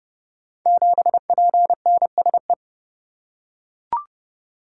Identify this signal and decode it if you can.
Morse code, heard as a keyed tone, about 30 wpm, 700 Hz. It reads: 7PNSE
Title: Code Groups - 5 Letter:Numbers - 30wpm